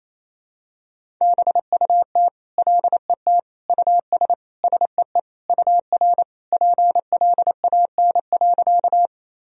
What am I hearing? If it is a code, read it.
Morse code, heard as a keyed tone, about 28 wpm, 700 Hz. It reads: BUT LET US SEE UR PLAN.
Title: Sentences from Top 500 Words and Abbreviations 28wpm